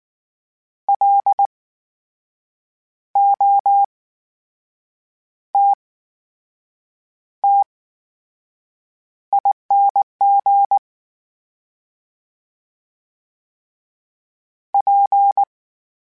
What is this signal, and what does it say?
Morse code, heard as a keyed tone, about 19 wpm, spaced as for 6 wpm, 800 Hz. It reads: LOTTING P
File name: Special INGs / Special INGs-mp3s / VerySlowToFaster Ing @18